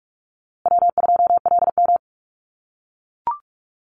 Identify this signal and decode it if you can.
Morse code, heard as a keyed tone, about 44 wpm, 700 Hz. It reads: W2LM
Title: USA Call Signs Repeated 3x 45wpm